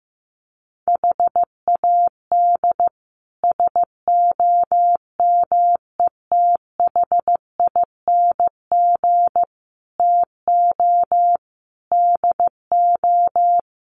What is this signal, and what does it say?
Morse code, heard as a keyed tone, about 15 wpm, 700 Hz. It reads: HAD SOMETHING TO DO
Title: Sets of 5 Words 15wpm